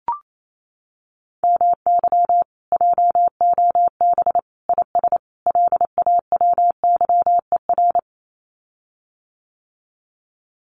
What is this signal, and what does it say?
Morse code, heard as a keyed tone, about 28 wpm, 700 Hz. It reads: MY JOB IS LAWYER
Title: Common QSO Phrases Encore 28wpm